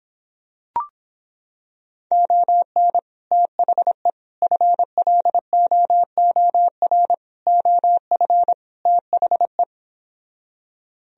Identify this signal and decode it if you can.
Morse code, heard as a keyed tone, about 26 wpm, 700 Hz. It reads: ON THE FLOOR OF THE